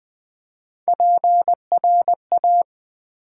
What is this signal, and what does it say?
Morse code, heard as a keyed tone, about 20 wpm, 700 Hz. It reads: PRA
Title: Build A Word - 3 Letters At A Time - 6-Letter Words - 20wpm